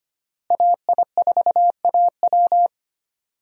Morse code, heard as a keyed tone, about 25 wpm, 700 Hz. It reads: AI4AW
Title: USA Call Signs - 5 Characters - Repeated 3x - 25wpm